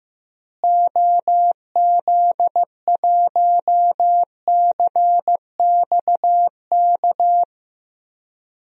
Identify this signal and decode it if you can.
Morse code, heard as a keyed tone, about 15 wpm, 700 Hz. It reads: OZ1CXK
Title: DXCC-100 Call Signs - 6 Characters - Repeated 3x - 15wpm